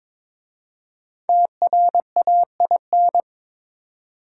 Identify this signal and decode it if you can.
Morse code, heard as a keyed tone, about 22 wpm, 700 Hz. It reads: TRAIN
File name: 5 Letter Words - First 3 Letters Given - 22wpm